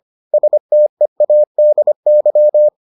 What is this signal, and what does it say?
Morse code, heard as a keyed tone, about 25 wpm, 600 Hz. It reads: STEADY